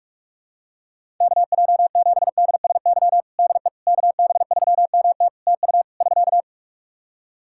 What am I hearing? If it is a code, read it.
Morse code, heard as a keyed tone, about 47 wpm, 700 Hz. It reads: KJ7DSY DE KB3MT TU <SK>
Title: K1USN SST QSO Practice 45wpm